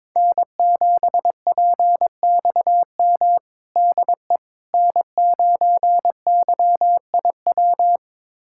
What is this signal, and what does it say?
Morse code, heard as a keyed tone, about 22 wpm, 700 Hz. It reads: N7PXM DE N9YIW
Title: USA Callsign DE Callsign - 5 Characters - 22wpm